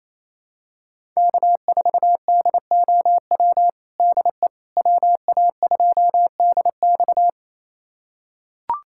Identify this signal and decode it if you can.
Morse code, heard as a keyed tone, about 28 wpm, 700 Hz. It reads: K4DOW DE WA2DX
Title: USA Callsign DE Callsign - 5 Characters - 28wpm